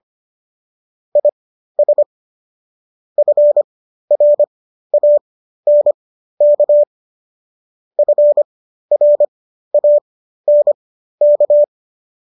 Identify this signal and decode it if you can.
Morse code, heard as a keyed tone, about 25 wpm, spaced as for 13 wpm, 590 Hz. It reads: IS FRANK FRANK